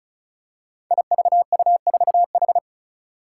Morse code, heard as a keyed tone, about 36 wpm, 700 Hz. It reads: IVU4H
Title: Five Challenging Letter-Number Combinations 35wpm